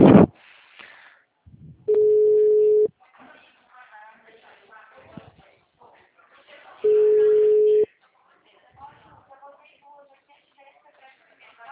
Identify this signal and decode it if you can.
Ringback tone (Europe, 425 Hz), with speech.
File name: out- -2007- - - .60828